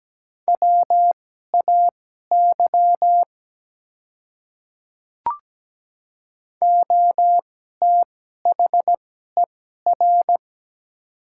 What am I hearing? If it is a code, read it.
Morse code, heard as a keyed tone, about 17 wpm, spaced as for 12 wpm, 700 Hz. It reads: WAY OTHER
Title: 2x Character Spacing - Top 100 Words - 17wpm